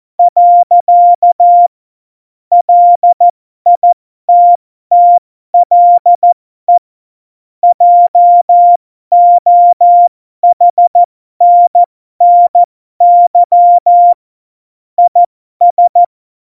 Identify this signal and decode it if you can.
Morse code, heard as a keyed tone, about 14 wpm, spaced as for 12 wpm, 700 Hz. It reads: . LITTLE JOHNNY IS
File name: Morsekurs / Witze